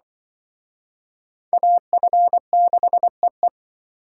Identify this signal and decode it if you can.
Morse code, heard as a keyed tone, about 25 wpm, 700 Hz. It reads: AF6EE